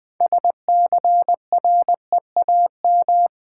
Morse code, heard as a keyed tone, about 20 wpm, 700 Hz. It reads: SCREAM
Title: Build A Word - 2 Letters At A Time - 6-Letter Words - 20wpm